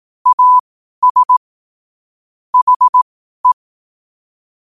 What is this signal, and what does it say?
Morse code, heard as a keyed tone, about 18 wpm, spaced as for 12 wpm, 1000 Hz. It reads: AS HE
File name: qotdfiles / 12wpm